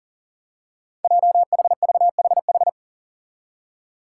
Morse code, heard as a keyed tone, about 38 wpm, 700 Hz. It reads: JHVHH
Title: Five Challenging Letter-Number Combinations 40wpm